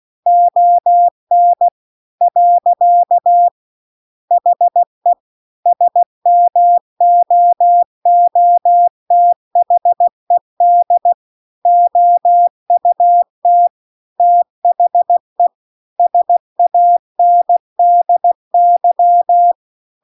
Morse code, heard as a keyed tone, about 16 wpm, 700 Hz. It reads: ON . HE SMOOTHED OUT THE SANDY